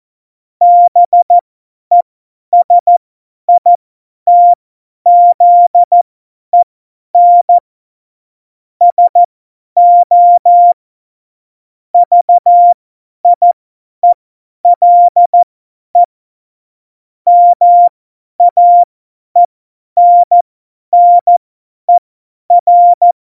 Morse code, heard as a keyed tone, about 14 wpm, spaced as for 10 wpm, 700 Hz. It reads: BESITZEN SO VIELE MAENNER
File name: Morsekurs / Witze / Witz 15. Oktober 2019, 50 F70